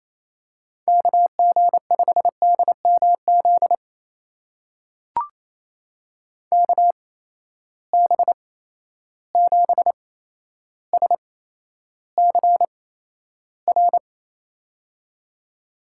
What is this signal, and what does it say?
Morse code, heard as a keyed tone, about 28 wpm, spaced as for 12 wpm, 700 Hz. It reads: KG5DMZ KB7SCR
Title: USA Call Signs - 6 Characters - 8x Character Spacing - 28wpm